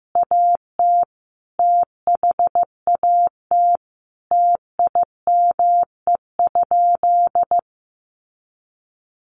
Morse code, heard as a keyed tone, about 15 wpm, 700 Hz. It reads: AT THAT TIME?
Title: Sentences from Top 200 Words 15wpm